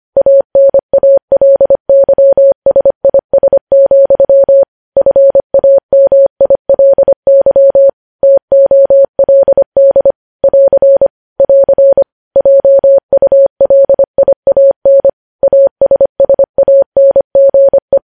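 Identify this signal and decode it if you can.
Morse code, heard as a keyed tone, about 25 wpm, 550 Hz. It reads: ANALYSIS, FAMILY TOLD <AR> <AR> JULIAN ASSANGE
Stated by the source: Morse news headlines sent at 25 wpm.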